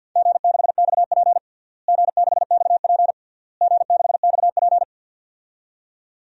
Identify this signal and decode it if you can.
Morse code, heard as a keyed tone, about 47 wpm, 700 Hz. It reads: G6XP G6XP G6XP
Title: DXCC-100 Call Signs - 4 Characters - Repeated 3x - 50wpm